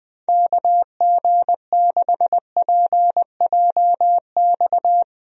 Morse code, heard as a keyed tone, about 21 wpm, 700 Hz. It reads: KG6PJX
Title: USA Call Signs - 6 Characters - Repeated 2x - 20wpm